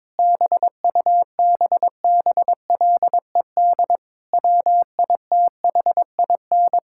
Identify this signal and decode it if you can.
Morse code, heard as a keyed tone, about 22 wpm, 700 Hz. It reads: BUBBLED WITHIN
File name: Difficult Sentences with Hints - 22wpm